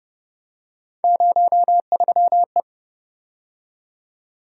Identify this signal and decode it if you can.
Morse code, heard as a keyed tone, about 32 wpm, 700 Hz. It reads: 03E